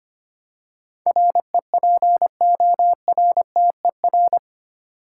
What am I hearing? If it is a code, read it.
Morse code, heard as a keyed tone, about 25 wpm, 700 Hz. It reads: REPORTER
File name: Eight Letter Words - 25wpm